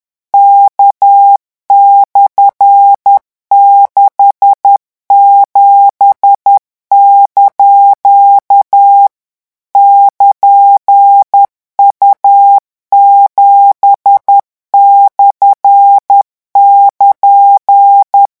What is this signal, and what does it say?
Morse code, heard as a keyed tone, about 10 wpm, 800 Hz. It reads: K/67) <KN>U7/<KN>
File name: morse mp3 / revision16-vitesse 10 mots